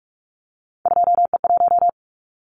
Morse code, heard as a keyed tone, about 46 wpm, 700 Hz. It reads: 2E1
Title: Callsign Prefixes for Top 25 DXCC - 45wpm